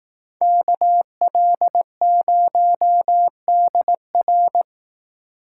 Morse code, heard as a keyed tone, about 19 wpm, 700 Hz. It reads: KL0DR
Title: Speed Racing - USA 2x2 Call Signs 15wpm